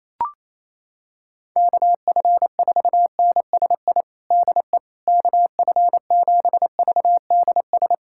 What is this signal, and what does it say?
Morse code, heard as a keyed tone, about 28 wpm, 700 Hz. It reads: KF4NSI DE KF7VDS
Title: USA Callsign DE Callsign - 6 Characters - 28wpm